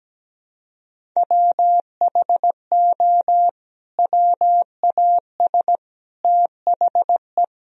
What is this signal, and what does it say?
Morse code, heard as a keyed tone, about 17 wpm, 700 Hz. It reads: WHO WAS THE
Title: Sentences from Top 300 Words and Abbreviations 17wpm